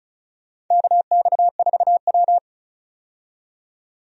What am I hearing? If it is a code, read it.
Morse code, heard as a keyed tone, about 36 wpm, 700 Hz. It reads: KX4W
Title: USA Call Signs - 4 Characters - 35wpm